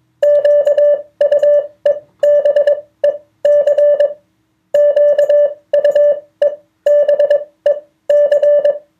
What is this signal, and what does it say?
Morse code, heard as a keyed tone, about 21 wpm, spaced as for 16 wpm, 580 Hz. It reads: QUEBEC QUEBEC